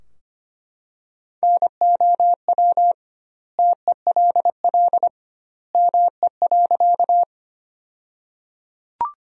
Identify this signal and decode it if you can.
Morse code, heard as a keyed tone, about 25 wpm, 700 Hz. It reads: NOW TELL ME.
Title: 2x Word-Spacing - Sentences from Top 200 Words - 25wpm